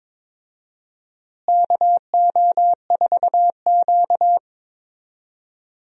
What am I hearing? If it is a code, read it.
Morse code, heard as a keyed tone, about 22 wpm, 700 Hz. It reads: KO4Q
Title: USA Call Signs - 4 Characters - Repeated 3x - 22wpm